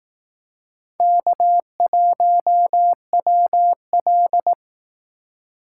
Morse code, heard as a keyed tone, about 18 wpm, 700 Hz. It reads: K1WL
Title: Speed Racing - USA Call Signs 15wpm